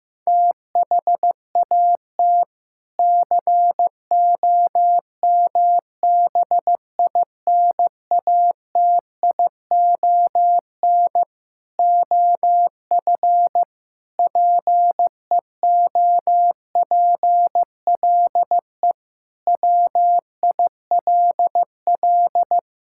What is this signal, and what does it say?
Morse code, heard as a keyed tone, about 16 wpm, 700 Hz. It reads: THAT COMBINATION OF PEOPLE WILL
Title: Sentences from Top 1000 Words - Review Part 6 - 15wpm